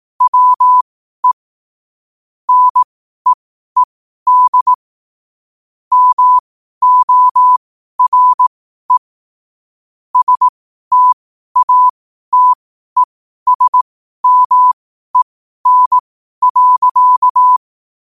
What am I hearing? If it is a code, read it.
Morse code, heard as a keyed tone, about 18 wpm, spaced as for 12 wpm, 1000 Hz. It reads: WE NEED MORE STATESMEN.
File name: qotdfiles / 12wpm